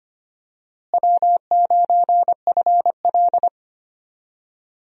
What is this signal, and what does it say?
Morse code, heard as a keyed tone, about 25 wpm, 700 Hz. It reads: W9FL
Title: USA Call Signs - 4 Characters - Repeated 2x - 25wpm